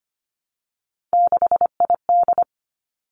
Morse code, heard as a keyed tone, about 25 wpm, 700 Hz. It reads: 6ID
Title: Three Challenging Letter-Number Combinations 25wpm